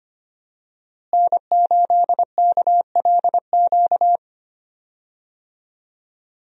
Morse code, heard as a keyed tone, about 25 wpm, 700 Hz. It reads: N8KLQ
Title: USA Call Signs - 5 Characters - Repeated 3x - 25wpm